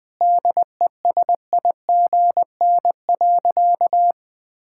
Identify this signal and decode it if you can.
Morse code, heard as a keyed tone, about 20 wpm, 700 Hz. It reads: DESIGN.
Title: Sentences from Top 1000 Words - Review Part 5 - 20wpm